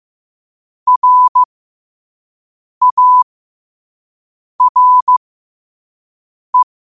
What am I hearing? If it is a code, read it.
Morse code, heard as a keyed tone, about 15 wpm, spaced as for 5 wpm, 1000 Hz. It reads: RARE